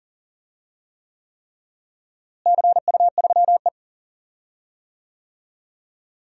Morse code, heard as a keyed tone, about 39 wpm, 700 Hz. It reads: CU3E
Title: DXCC-100 Call Signs - 4 Characters - Repeated 3x - 40wpm